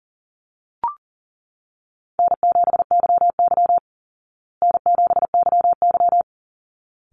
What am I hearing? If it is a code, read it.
Morse code, heard as a keyed tone, about 40 wpm, 700 Hz. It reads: N7YY N7YY
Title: Easy Call Signs Repeated 2x - 40wpm